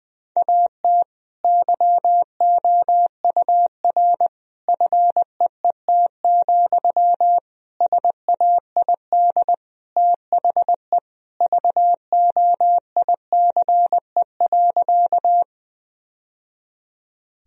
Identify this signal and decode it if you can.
Morse code, heard as a keyed tone, about 21 wpm, 700 Hz. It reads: AT YOUR FEET, SAID THE VOICE.
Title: Sentences from Top 1000 Words - Review Part 1 - 20wpm